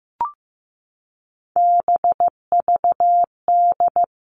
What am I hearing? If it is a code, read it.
Morse code, heard as a keyed tone, about 15 wpm, 700 Hz. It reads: BVD
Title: Three Challenging Letter-Number Combinations 15wpm